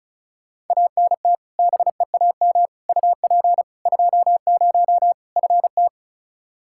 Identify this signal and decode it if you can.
Morse code, heard as a keyed tone, about 34 wpm, 700 Hz. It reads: ANT BEAM UP 20 FT
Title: K1USN SST QSO Practice 35wpm